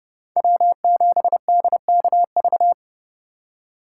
Morse code, heard as a keyed tone, about 39 wpm, 700 Hz. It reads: W7DKV